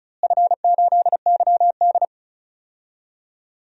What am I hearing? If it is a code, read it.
Morse code, heard as a keyed tone, about 34 wpm, 700 Hz. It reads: F8YD
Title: DXCC-100 Call Signs - 4 Characters - Repeated 2x - 35wpm